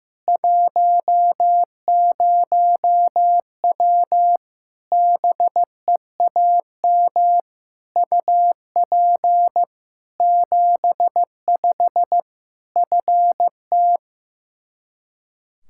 Morse code, heard as a keyed tone, about 15 wpm, 700 Hz. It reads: 10W BEAM UP 75 FT
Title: Practice QSO - Exchange 2 - Info - 15wpm